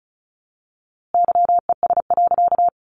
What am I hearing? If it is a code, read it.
Morse code, heard as a keyed tone, about 34 wpm, 700 Hz. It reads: YES.
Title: A Sound Of Thunder 35wpm